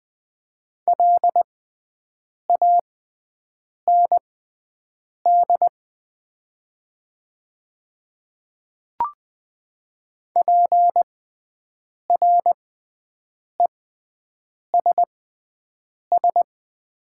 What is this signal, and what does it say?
Morse code, heard as a keyed tone, about 20 wpm, spaced as for 7 wpm, 700 Hz. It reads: LAND PRESS
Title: 6x Character Spacing - Top 300 Words - 20wpm